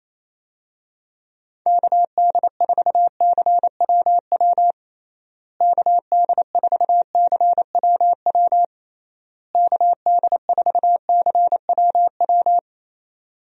Morse code, heard as a keyed tone, about 28 wpm, 700 Hz. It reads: KD4CWW KD4CWW KD4CWW
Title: USA Call Signs - 6 Characters - Repeated 3x - 28wpm